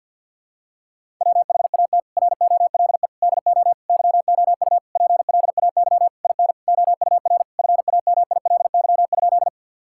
Morse code, heard as a keyed tone, about 52 wpm, 700 Hz. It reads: WHAT ROLE DO YOU PLAY IN OUR FAMILY?
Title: Sentences from Top 1000 Words - Review Part 6 - 50wpm